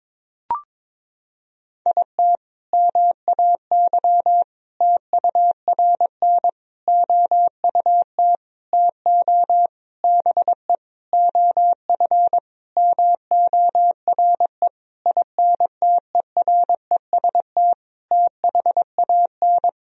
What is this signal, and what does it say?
Morse code, heard as a keyed tone, about 22 wpm, 700 Hz. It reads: IT MAY TURN OUT TO BE OF MORE INTEREST THAN
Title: Sentences from Top 1000 Words - Review Part 2 - 22wpm